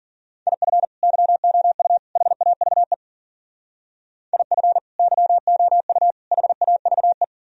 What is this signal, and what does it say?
Morse code, heard as a keyed tone, about 49 wpm, 700 Hz. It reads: IF YOU HAVE IF YOU HAVE
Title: Speed-Racing - Sets of 3 Words 40wpm